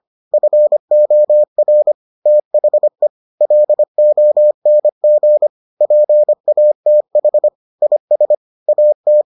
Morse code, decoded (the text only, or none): FOR THE LONG PATH IS AT